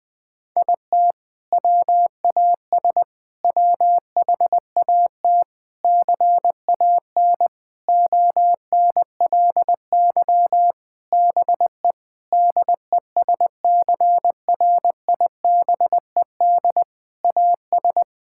Morse code, heard as a keyed tone, about 20 wpm, 700 Hz. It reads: IT WAS WHAT CAN ONLY BE DESCRIBED AS
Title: Sentences with 6-Word Phrases - 20wpm